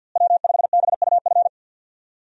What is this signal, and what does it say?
Morse code, heard as a keyed tone, about 50 wpm, 700 Hz. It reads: W5BUF
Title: USA Call Signs - 5 Characters - Repeated 3x - 50wpm